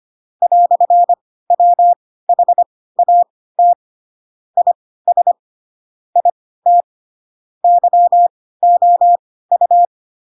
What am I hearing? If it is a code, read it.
Morse code, heard as a keyed tone, about 25 wpm, spaced as for 16 wpm, 700 Hz. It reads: "WHAT IS IT YOU
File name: Pinocchio / Pinocchio ch2 10 25 ik1hge